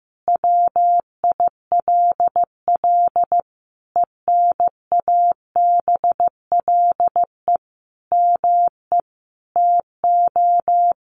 Morse code, heard as a keyed tone, about 15 wpm, 700 Hz. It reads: WILL ENABLE ME TO
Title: Sentences from Top 1500 Words 15wpm